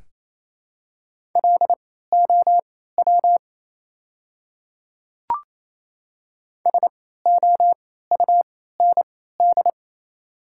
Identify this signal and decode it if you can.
Morse code, heard as a keyed tone, about 28 wpm, spaced as for 16 wpm, 700 Hz. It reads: LOW SOUND
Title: 3x Character Spacing - Top 200 Words - 28wpm